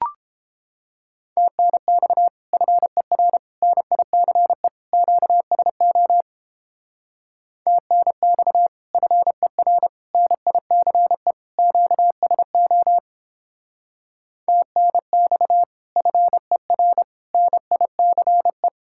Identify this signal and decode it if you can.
Morse code, heard as a keyed tone, about 31 wpm, 700 Hz. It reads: TNX FER NICE QSO TNX FER NICE QSO TNX FER NICE